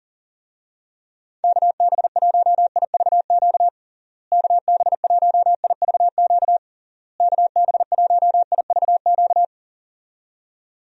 Morse code, heard as a keyed tone, about 40 wpm, 700 Hz. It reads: KB1IVQ KB1IVQ KB1IVQ
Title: USA Call Signs - 6 Characters - Repeated 3x - 40wpm